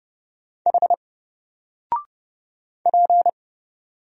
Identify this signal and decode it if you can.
Morse code, heard as a keyed tone, about 30 wpm, 700 Hz. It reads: H P
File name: Single-Letter-Number-30wpm